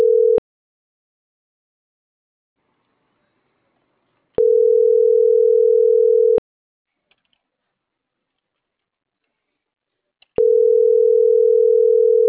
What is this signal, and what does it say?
Ringback tone (North America), 440 + 480 Hz.